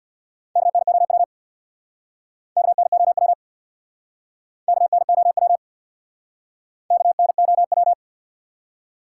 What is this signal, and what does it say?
Morse code, heard as a keyed tone, about 69 wpm, 700 Hz. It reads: KNOW KNOW KNOW KNOW